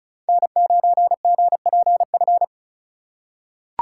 Morse code, heard as a keyed tone, about 36 wpm, 700 Hz. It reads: N9GPF
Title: USA Call Signs - 5 Characters - 2x Character Spacing - 35wpm